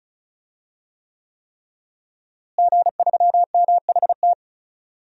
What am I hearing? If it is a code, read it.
Morse code, heard as a keyed tone, about 36 wpm, 700 Hz. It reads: G3MHT
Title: DXCC-100 Call Signs - 5 Characters - Repeated 3x - 35wpm